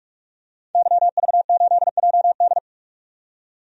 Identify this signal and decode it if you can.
Morse code, heard as a keyed tone, about 47 wpm, 700 Hz. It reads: YV8JD
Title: DXCC-100 Call Signs - 5 Characters - 45wpm